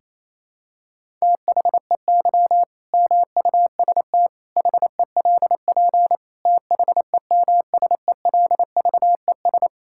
Morse code, heard as a keyed tone, about 28 wpm, 700 Hz. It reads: THEY MUST HELP THEMSELVES
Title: Sentences from Top 1000 Words - Review Part 5 - 28wpm